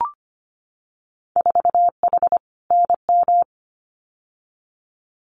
Morse code, heard as a keyed tone, about 25 wpm, 700 Hz. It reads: 4H NM
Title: Winter Field Day Exchanges Encore 25wpm